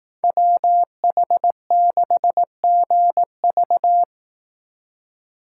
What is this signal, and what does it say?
Morse code, heard as a keyed tone, about 18 wpm, 700 Hz. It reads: WH6GV
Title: Speed Racing - USA 2x2 Call Signs 15wpm